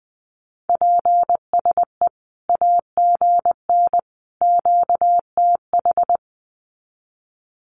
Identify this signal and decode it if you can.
Morse code, heard as a keyed tone, about 20 wpm, 700 Hz. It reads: PSE AGN QTH
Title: Common QSO Phrases 20wpm